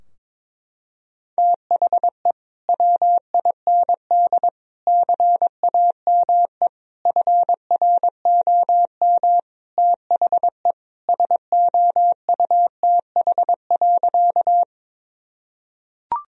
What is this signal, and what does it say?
Morse code, heard as a keyed tone, about 21 wpm, 700 Hz. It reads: THE WIND CAME FROM THE SOUTH.
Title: Sentences from Top 1000 Words - Review Part 2 - 22wpm